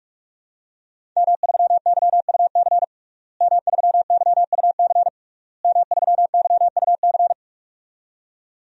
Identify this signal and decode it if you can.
Morse code, heard as a keyed tone, about 42 wpm, 700 Hz. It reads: M3YUC M3YUC M3YUC
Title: DXCC-100 Call Signs - 5 Characters - Repeated 3x - 45wpm